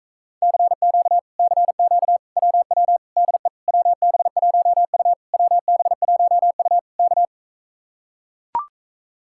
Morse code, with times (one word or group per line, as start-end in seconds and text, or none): 0.4-1.2 CQ
1.4-2.2 CQ
2.4-3.0 WW
3.2-3.5 DE
3.7-5.1 WB1U
5.3-6.8 WB1U
7.0-7.2 K